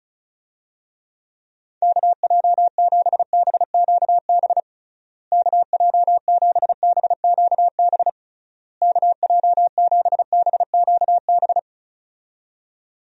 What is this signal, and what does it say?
Morse code, heard as a keyed tone, about 36 wpm, 700 Hz. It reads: KJ7BQB KJ7BQB KJ7BQB
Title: USA Call Signs - 6 Characters - Repeated 3x - 35wpm